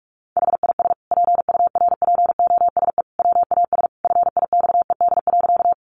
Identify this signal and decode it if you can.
Morse code, heard as a keyed tone, about 44 wpm, 700 Hz. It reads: HIS PURPOSE WAS FIXED.
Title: Sentences from Top 1100 Words 45wpm